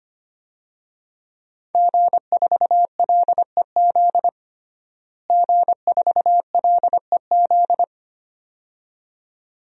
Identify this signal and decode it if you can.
Morse code, heard as a keyed tone, about 25 wpm, 700 Hz. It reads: G4LEZ G4LEZ
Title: DXCC-100 Call Signs - 5 Characters - Repeated 2x - 25wpm